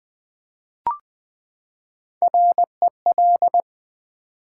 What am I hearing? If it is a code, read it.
Morse code, heard as a keyed tone, about 20 wpm, 700 Hz. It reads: REL
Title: Build A Word - 3 Letters At A Time - 6-Letter Words - 20wpm